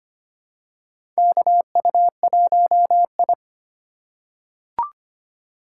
Morse code, heard as a keyed tone, about 25 wpm, 700 Hz. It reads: KU1I
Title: USA Call Signs - 4 Characters - 2x Character Spacing - 25wpm